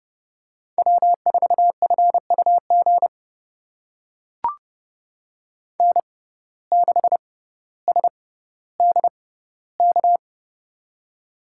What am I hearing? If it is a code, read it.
Morse code, heard as a keyed tone, about 30 wpm, spaced as for 15 wpm, 700 Hz. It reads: W4FUG N6SDK
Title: USA Call Signs - 5 Characters - 6x Character Spacing - 30wpm